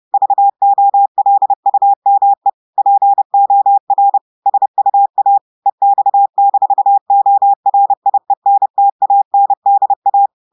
Morse code, heard as a keyed tone, about 30 wpm, 800 Hz. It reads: VOLUME POR SUA EX-ORIENTANDA
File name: audios / portugais / portugais texte en clair 1 vitesse 30 mots minutes